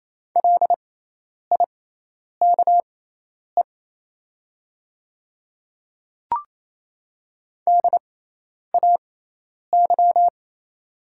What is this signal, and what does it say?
Morse code, heard as a keyed tone, about 28 wpm, spaced as for 10 wpm, 700 Hz. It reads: LIKE DAY
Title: 6x Character Spacing - Top 100 Words - 28wpm